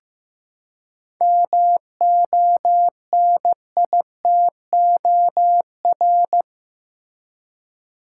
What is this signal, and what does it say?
Morse code, heard as a keyed tone, about 15 wpm, 700 Hz. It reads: MONITOR
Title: Seven Letter Words - 15wpm